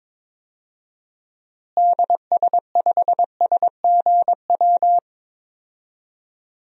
Morse code, heard as a keyed tone, about 22 wpm, 700 Hz. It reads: DS5SGW
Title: DXCC-100 Call Signs - 6 Characters - Repeated 3x - 22wpm